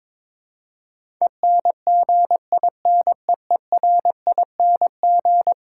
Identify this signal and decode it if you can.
Morse code, heard as a keyed tone, about 22 wpm, 700 Hz. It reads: ENGINEERING
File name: practice / 22wpm / Eleven Letter Words - 22wpm